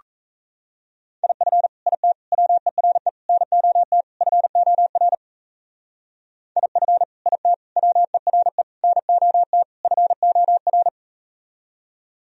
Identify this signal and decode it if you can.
Morse code, heard as a keyed tone, about 38 wpm, 700 Hz. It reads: IF IT WERE NOT FOR IF IT WERE NOT FOR